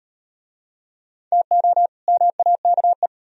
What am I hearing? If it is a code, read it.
Morse code, heard as a keyed tone, about 38 wpm, 700 Hz. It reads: TO MAKE